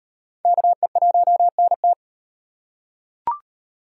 Morse code, heard as a keyed tone, about 38 wpm, 700 Hz. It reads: KE1NT